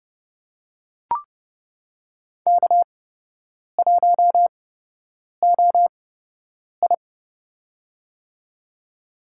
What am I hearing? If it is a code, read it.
Morse code, heard as a keyed tone, about 30 wpm, spaced as for 8 wpm, 700 Hz. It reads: K1OI
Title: USA Call Signs - 4 Characters - 8x Character Spacing - 30wpm